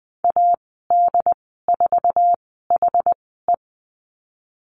Morse code, heard as a keyed tone, about 20 wpm, 700 Hz. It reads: A D 4 H E
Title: 154 Lesson 25g - Letter p - Call Signs 20wpm